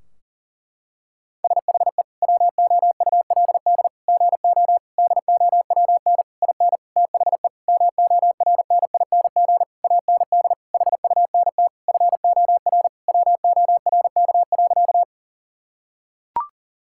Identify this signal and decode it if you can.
Morse code, heard as a keyed tone, about 40 wpm, 700 Hz. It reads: SHE WOULD GO DOWN IN THE MORNING AND HUNT FOR WORK.
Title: Sentences from Top 1600 Words 40wpm